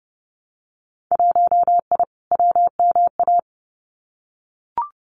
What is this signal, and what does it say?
Morse code, heard as a keyed tone, about 30 wpm, 700 Hz. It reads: 1I WMA
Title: Winter Field Day Exchanges Encore 30wpm